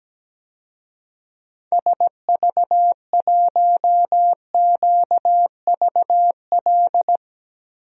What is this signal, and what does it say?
Morse code, heard as a keyed tone, about 17 wpm, 700 Hz. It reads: SV1QVL
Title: DXCC-100 Call Signs - 6 Characters - Repeated 2x - 17wpm